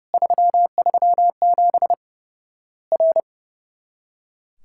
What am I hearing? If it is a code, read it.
Morse code, heard as a keyed tone, about 30 wpm, 700 Hz. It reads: 337 R
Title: Worked All Europe DX Contest - QTC Exchange - 30wpm